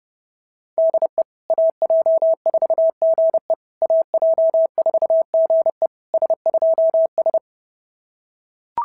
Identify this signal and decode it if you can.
Morse code, heard as a keyed tone, about 30 wpm, 690 Hz. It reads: DE AJ4GE AJ4GE S2S
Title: SOTA QSO Practice 30wpm